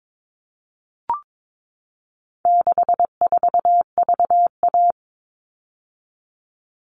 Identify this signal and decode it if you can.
Morse code, heard as a keyed tone, about 22 wpm, 700 Hz. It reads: 64VA
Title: All Cut Numbers - 4-Digit 22wpm